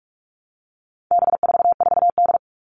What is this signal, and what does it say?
Morse code, heard as a keyed tone, about 44 wpm, 700 Hz. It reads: B44D
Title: All Cut Numbers - 4-Digit 45wpm